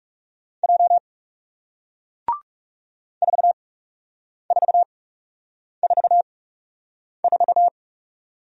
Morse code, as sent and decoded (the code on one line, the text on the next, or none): .--- / ....- ....- ....- ....-
J 4444